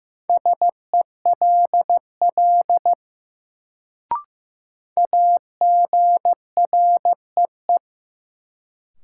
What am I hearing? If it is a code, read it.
Morse code, heard as a keyed tone, about 15 wpm, 700 Hz. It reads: SELL AGREE
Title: Sentences from Top 1000 Words - Review Part 6 - New Word Drill - 15wpm